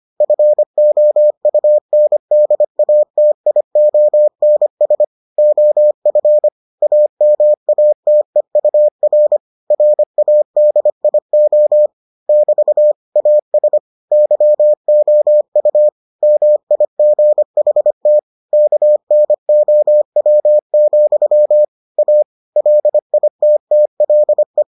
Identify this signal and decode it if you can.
Morse code, heard as a keyed tone, about 25 wpm, 600 Hz. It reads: FOUNDATIONS OF AMATEUR RADIO <BT> AS YOU MIGHT KNOW, A LITTLE